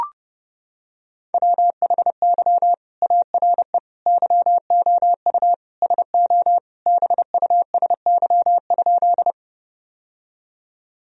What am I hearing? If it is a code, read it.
Morse code, heard as a keyed tone, about 30 wpm, 700 Hz. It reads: WHY ARE YOU SO BUSY?
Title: Sentences from Top 1000 Words - Review Part 3 - 30wpm